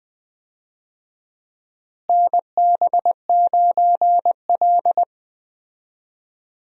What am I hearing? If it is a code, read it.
Morse code, heard as a keyed tone, about 20 wpm, 700 Hz. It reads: NB9L
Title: USA Call Signs - 4 Characters - Repeated 2x - 20wpm